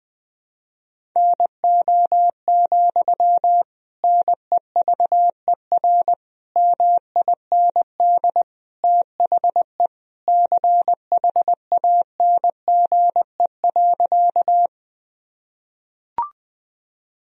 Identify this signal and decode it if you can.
Morse code, heard as a keyed tone, about 20 wpm, 700 Hz. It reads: NO, NEVER MIND THE CHANGE.
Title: Sentences from Top 1000 Words - Review Part 1 - 20wpm